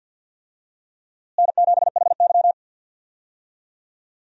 Morse code, heard as a keyed tone, about 53 wpm, 700 Hz. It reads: N7HY
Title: USA Call Signs - 4 Characters - Repeated 2x - 50wpm